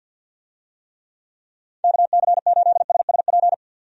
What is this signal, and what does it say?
Morse code, heard as a keyed tone, about 47 wpm, 700 Hz. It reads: KC8SSP
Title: USA Call Signs - 6 Characters - 50wpm